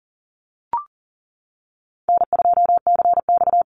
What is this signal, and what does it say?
Morse code, heard as a keyed tone, about 40 wpm, 700 Hz. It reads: N2CX
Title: Call Signs - 40wpm